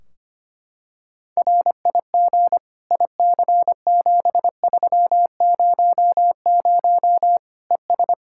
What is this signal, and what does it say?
Morse code, heard as a keyed tone, about 25 wpm, 700 Hz. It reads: RIG IC7300 ES
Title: Practice QSO - Exchange 2 - Info - 25wpm